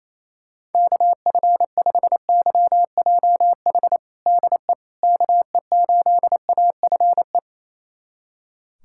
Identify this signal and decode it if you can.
Morse code, heard as a keyed tone, about 28 wpm, 700 Hz. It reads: KF5YJH DE KE8AFE
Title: USA Callsign DE Callsign - 6 Characters - 28wpm